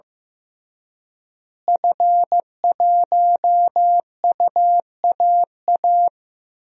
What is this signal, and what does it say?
Morse code, heard as a keyed tone, about 15 wpm, 700 Hz. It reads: F1UAA
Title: DXCC-100 Call Signs - 5 Characters - Repeated 3x - 15wpm